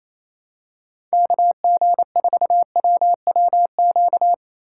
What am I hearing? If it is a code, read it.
Morse code, heard as a keyed tone, about 28 wpm, 700 Hz. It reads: KG4WWQ
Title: USA Call Signs - 6 Characters - 3x Character Spacing - 28wpm